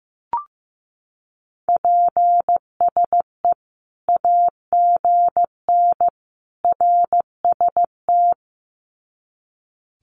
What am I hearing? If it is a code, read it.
Morse code, heard as a keyed tone, about 15 wpm, 700 Hz. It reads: PSE AGN RST
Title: Common QSO Phrases 15wpm